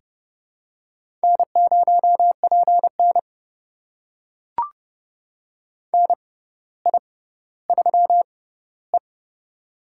Morse code, heard as a keyed tone, about 31 wpm, spaced as for 15 wpm, 700 Hz. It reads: N0PN NI3E
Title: USA Call Signs - 4 Characters - 6x Character Spacing - 30wpm